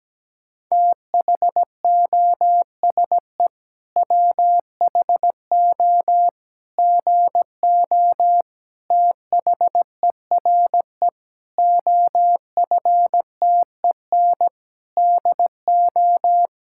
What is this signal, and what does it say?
Morse code, heard as a keyed tone, about 17 wpm, 700 Hz. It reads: THOSE WHO GO THERE OFTEN DO
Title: Sentences from Top 1000 Words - Review Part 5 - 17wpm